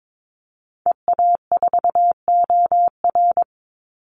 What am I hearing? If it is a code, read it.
Morse code, heard as a keyed tone, about 22 wpm, 700 Hz. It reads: EA4OR
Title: Call Signs Repeated 2x - 22wpm